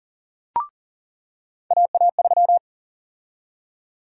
Morse code, heard as a keyed tone, about 38 wpm, 700 Hz. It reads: AA3
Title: USA Call Sign Prefixes - 40wpm